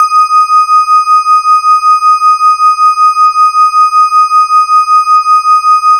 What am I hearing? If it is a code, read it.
There is no code.